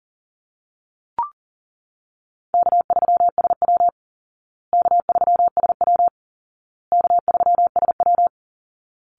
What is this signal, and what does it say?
Morse code, heard as a keyed tone, about 40 wpm, 700 Hz. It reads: K3SW K3SW K3SW
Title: Easy Call Signs Repeated 3x - 40wpm